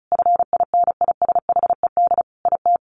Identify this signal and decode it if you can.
Morse code, heard as a keyed tone, about 34 wpm, 700 Hz. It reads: FINISHED IT
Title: A Sound Of Thunder 35wpm